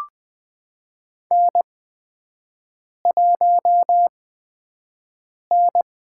Morse code, heard as a keyed tone, about 20 wpm, 700 Hz. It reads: N 1 N
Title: USA Call Signs - 5 Characters - 8x Character Spacing - 20wpm